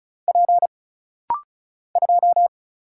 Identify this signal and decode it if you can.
Morse code, heard as a keyed tone, about 36 wpm, 700 Hz. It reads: P 2
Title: Single Letter-Number - Rapid-Fire 35wpm